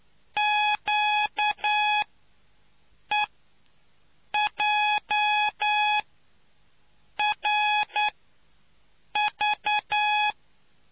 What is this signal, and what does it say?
Morse code, heard as a keyed tone, about 9 wpm, spaced as for 5 wpm, 840 Hz. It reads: QEJRV